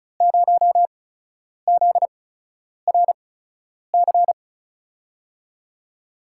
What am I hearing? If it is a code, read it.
Morse code, heard as a keyed tone, about 36 wpm, spaced as for 10 wpm, 700 Hz. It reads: 0ZRC
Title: USA Call Signs - 6 Characters - 8x Character Spacing - 35wpm